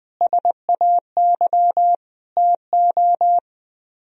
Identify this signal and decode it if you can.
Morse code, heard as a keyed tone, about 17 wpm, 700 Hz. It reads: SAY TO
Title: Speed-Racing - Sets of 5 Words 17wpm